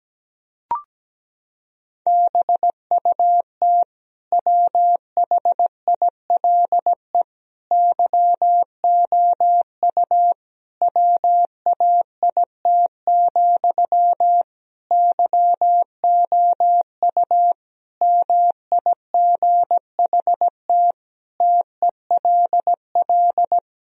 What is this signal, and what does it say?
Morse code, heard as a keyed tone, about 17 wpm, 700 Hz. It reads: BUT WHILE YOU WAIT, YOU MIGHT TELL
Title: Sentences from Top 1000 Words - Review Part 1 - 17wpm